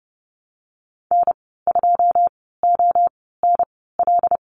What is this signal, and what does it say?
Morse code, heard as a keyed tone, about 30 wpm, spaced as for 17 wpm, 700 Hz. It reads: N2ONL
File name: Lesson 14g - Number 2 - Call Signs 30wpm